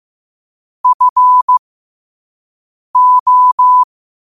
Morse code, heard as a keyed tone, about 15 wpm, spaced as for 5 wpm, 1000 Hz. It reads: FO
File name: qotdfiles / 5wpm